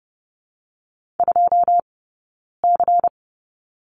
Morse code, heard as a keyed tone, about 30 wpm, 700 Hz. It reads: 2 C
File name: course / 30:10wpm / 099 Lesson 17d - Letter u - Two Characters 30wpm